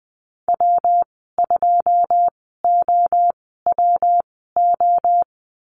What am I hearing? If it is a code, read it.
Morse code, heard as a keyed tone, about 19 wpm, spaced as for 14 wpm, 700 Hz. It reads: W2OWO
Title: 173 Lesson 28g - Number 9 - Call Signs 20wpm